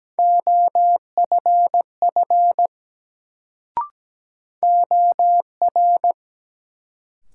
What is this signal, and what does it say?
Morse code, heard as a keyed tone, about 17 wpm, 700 Hz. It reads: OFF OR
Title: Sets of 2 Words - New Word Drill - 17wpm